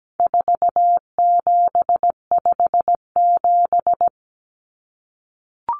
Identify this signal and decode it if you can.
Morse code, heard as a keyed tone, about 17 wpm, 700 Hz. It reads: 4757
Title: Numbers - 4-Digit 17wpm